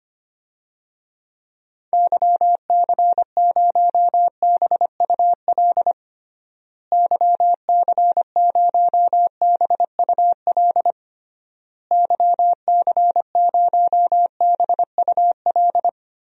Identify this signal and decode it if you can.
Morse code, heard as a keyed tone, about 25 wpm, 700 Hz. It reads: YC0BUL YC0BUL YC0BUL
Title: DXCC-100 Call Signs - 6 Characters - Repeated 3x - 25wpm